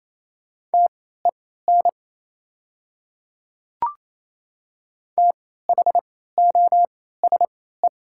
Morse code, heard as a keyed tone, about 28 wpm, spaced as for 16 wpm, 690 Hz. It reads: TEN THOSE
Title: 3x Character Spacing - Top 500 Words - 28wpm